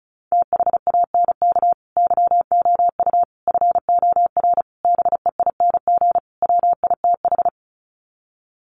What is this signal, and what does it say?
Morse code, heard as a keyed tone, about 34 wpm, 700 Hz. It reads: THANK YOU FOR BEING WITH
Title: Sets of 5 Words 35wpm